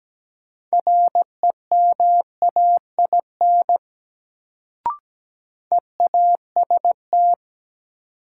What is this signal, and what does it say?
Morse code, heard as a keyed tone, about 17 wpm, 700 Hz. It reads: REMAIN EAST
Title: Sentences from Top 1200 Words - New Word Drill Alt - 17wpm